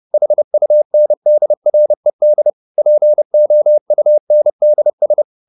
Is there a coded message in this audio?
Yes: Morse code from 0.1 to 5.2 s, heard as a keyed tone, about 30 wpm, 600 Hz. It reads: HUNDRED POUNDS